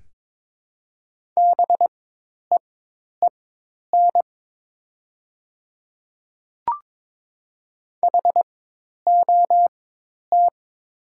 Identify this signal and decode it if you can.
Morse code, heard as a keyed tone, about 22 wpm, spaced as for 10 wpm, 700 Hz. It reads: BEEN HOT
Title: 4x Character Spacing - Top 100 Words - 22wpm